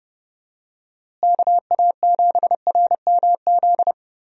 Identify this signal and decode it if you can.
Morse code, heard as a keyed tone, about 32 wpm, 700 Hz. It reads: KA7RMZ
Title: USA Call Signs - 6 Characters - Repeated 2x - 30wpm